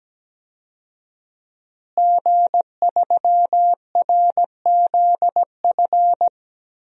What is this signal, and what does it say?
Morse code, heard as a keyed tone, about 17 wpm, 700 Hz. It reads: G3RZF
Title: DXCC-100 Call Signs - 5 Characters - Repeated 2x - 17wpm